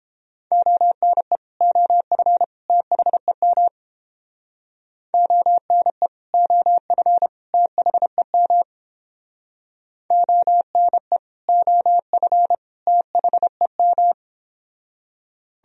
Morse code, heard as a keyed tone, about 29 wpm, 700 Hz. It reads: ONE OF THEM ONE OF THEM ONE OF THEM